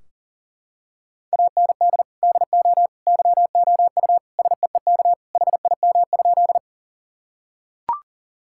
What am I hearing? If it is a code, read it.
Morse code, heard as a keyed tone, about 40 wpm, 700 Hz. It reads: AND DO YOU SEEK HIM?
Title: Sentences from Top 1600 Words 40wpm